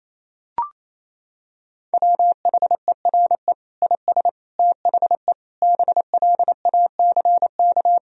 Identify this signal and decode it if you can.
Morse code, heard as a keyed tone, about 28 wpm, 700 Hz. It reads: WHERE IS THE BLACK
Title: Sentences from Top 1000 Words - Review Part 2 - 28wpm